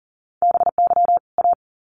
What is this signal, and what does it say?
Morse code, heard as a keyed tone, about 40 wpm, 700 Hz. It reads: BY A
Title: Sets of 2 Words 40wpm